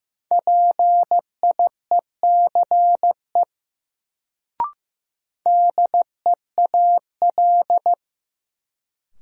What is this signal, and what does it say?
Morse code, heard as a keyed tone, about 15 wpm, 700 Hz. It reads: PIECE DEAL
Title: Sets of 3 Words - New Word Drill - 15wpm